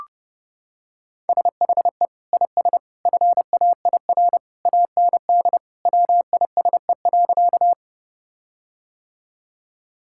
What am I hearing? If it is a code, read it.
Morse code, heard as a keyed tone, about 30 wpm, 700 Hz. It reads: SHE IS FAIR AND WISE.
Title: Sentences from Top 1600 Words 30wpm